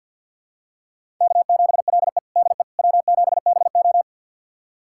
Morse code, heard as a keyed tone, about 52 wpm, 700 Hz. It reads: K7LE DE W7BO
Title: USA Callsign DE Callsign - 4 Characters - 50wpm